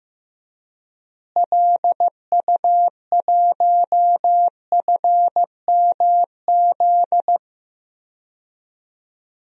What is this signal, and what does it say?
Morse code, heard as a keyed tone, about 15 wpm, 700 Hz. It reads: LU1FMZ
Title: DXCC-100 Call Signs - 6 Characters - Repeated 3x - 15wpm